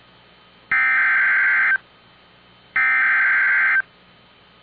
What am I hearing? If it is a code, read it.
Emergency Alert System (SAME) tones (header bursts), heard as harsh buzzing bursts. It is heard from 0.7 to 3.8 s.